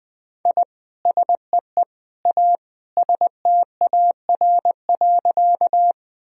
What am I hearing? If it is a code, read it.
Morse code, heard as a keyed tone, about 20 wpm, 700 Hz. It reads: I SEE A STAR.
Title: Sentences from Top 1000 Words - Review Part 2 - 20wpm